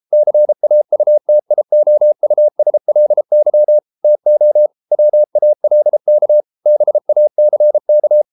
Morse code, heard as a keyed tone, about 33 wpm, 600 Hz. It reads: CAUTIOUSLY TO WALK BACK